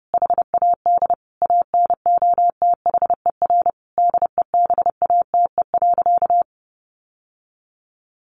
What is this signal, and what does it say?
Morse code, heard as a keyed tone, about 28 wpm, 700 Hz. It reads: HAD ANOTHER DEBATE.
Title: Sentences from Top 1100 Words 30wpm